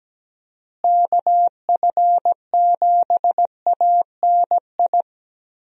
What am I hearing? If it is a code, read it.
Morse code, heard as a keyed tone, about 17 wpm, 700 Hz. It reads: KF7ANI
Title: USA Call Signs - 6 Characters - Repeated 2x - 17wpm